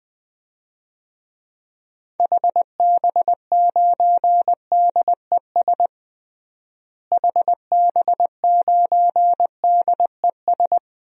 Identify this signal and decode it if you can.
Morse code, heard as a keyed tone, about 21 wpm, 700 Hz. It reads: HB9DES HB9DES
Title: DXCC-100 Call Signs - 6 Characters - Repeated 2x - 20wpm